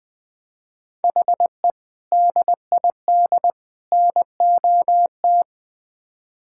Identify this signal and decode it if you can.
Morse code, heard as a keyed tone, about 21 wpm, 700 Hz. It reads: HE DID NOT
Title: Build A Phrase - 5-Word Phrases - 20wpm